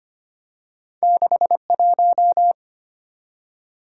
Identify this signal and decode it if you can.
Morse code, heard as a keyed tone, about 25 wpm, 700 Hz. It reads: 61
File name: Two Letter-Number Combinations - 25wpm